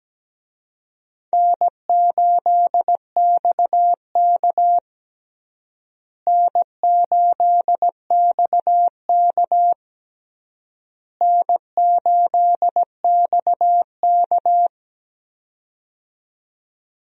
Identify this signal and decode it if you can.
Morse code, heard as a keyed tone, about 17 wpm, 700 Hz. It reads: N8XK N8XK N8XK
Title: USA Call Signs - 4 Characters - Repeated 3x - 17wpm